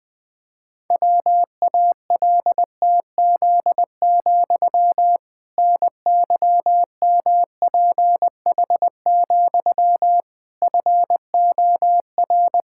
Morse code, heard as a keyed tone, about 20 wpm, 700 Hz. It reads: WALTZ, NYMPH, FOR
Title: Pangram Challenge v2 - 20wpm